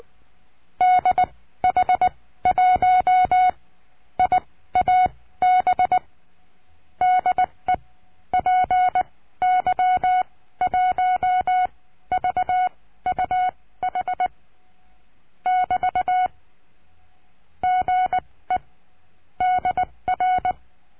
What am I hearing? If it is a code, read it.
Morse code, heard as a keyed tone, about 20 wpm, spaced as for 14 wpm, 740 Hz. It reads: DH1IAB DE PY1VUH <BT> GE DR
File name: CURSO DE TELEGRAFIA